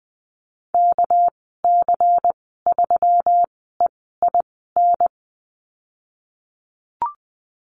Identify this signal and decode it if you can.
Morse code, heard as a keyed tone, about 20 wpm, spaced as for 15 wpm, 700 Hz. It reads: KC3EIN
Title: 213 Lesson 34g - Number 8 - Call Signs 20wpm